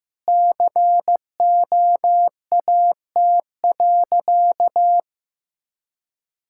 Morse code, heard as a keyed tone, about 15 wpm, 700 Hz. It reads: COAT.
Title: Sentences from Top 1600 Words 15wpm